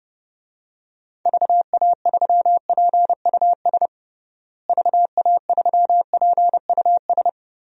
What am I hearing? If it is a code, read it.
Morse code, heard as a keyed tone, about 30 wpm, 700 Hz. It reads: VA3PUS VA3PUS
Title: DXCC-100 Call Signs - 6 Characters - Repeated 2x - 30wpm